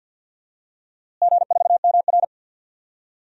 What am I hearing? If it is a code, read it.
Morse code, heard as a keyed tone, about 51 wpm, 700 Hz. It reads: G4MR
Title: DXCC-100 Call Signs - 4 Characters - Repeated 3x - 50wpm